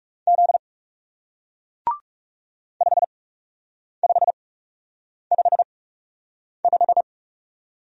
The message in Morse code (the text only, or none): Z 5555